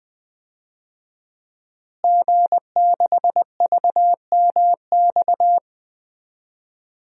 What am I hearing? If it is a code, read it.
Morse code, heard as a keyed tone, about 21 wpm, 700 Hz. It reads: G6VMX
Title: DXCC-100 Call Signs - 5 Characters - Repeated 2x - 20wpm